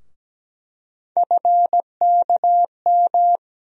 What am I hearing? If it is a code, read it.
Morse code, heard as a keyed tone, about 17 wpm, 700 Hz. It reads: FKM
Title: USA Call Sign Postfixes - 3 Letters 17wpm